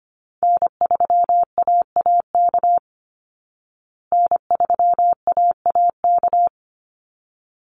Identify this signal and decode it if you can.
Morse code, heard as a keyed tone, about 25 wpm, 700 Hz. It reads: N3AAK N3AAK
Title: Call Signs Repeated 3x - 25wpm